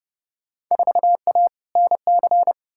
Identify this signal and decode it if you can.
Morse code, heard as a keyed tone, about 30 wpm, 700 Hz. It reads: 4A NC
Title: Field Day Exchanges - 30wpm